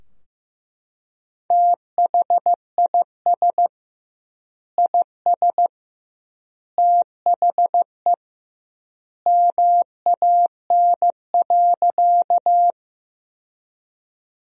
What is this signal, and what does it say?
Morse code, heard as a keyed tone, about 15 wpm, 700 Hz. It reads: THIS IS THE MAN.
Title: 2x Word-Spacing - Sentences from Top 200 Words - 15wpm